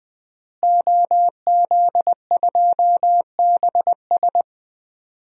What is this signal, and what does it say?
Morse code, heard as a keyed tone, about 21 wpm, 700 Hz. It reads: OZ2BS
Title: DXCC-100 Call Signs - 5 Characters - Repeated 2x - 20wpm